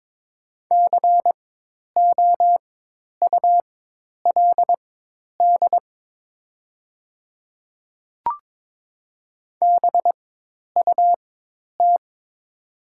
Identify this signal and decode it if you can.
Morse code, heard as a keyed tone, about 22 wpm, spaced as for 10 wpm, 700 Hz. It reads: COULD BUT
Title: 4x Character Spacing - Top 100 Words - 22wpm